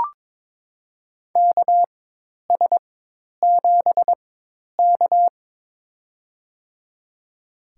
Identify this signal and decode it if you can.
Morse code, heard as a keyed tone, about 21 wpm, spaced as for 10 wpm, 700 Hz. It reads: KS7K
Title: USA Call Signs - 4x Character Spacing - 22wpm